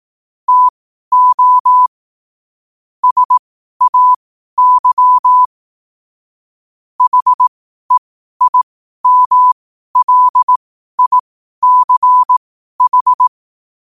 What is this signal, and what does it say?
Morse code, heard as a keyed tone, about 18 wpm, spaced as for 12 wpm, 1000 Hz. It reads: TO SAY HEIMLICH
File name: qotdfiles / 12wpm